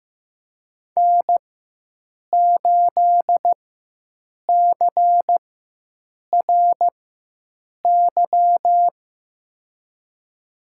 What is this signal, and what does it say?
Morse code, heard as a keyed tone, about 15 wpm, spaced as for 7 wpm, 700 Hz. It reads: N8CRY
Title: USA Call Signs - 5 Characters - 4x Character Spacing - 15wpm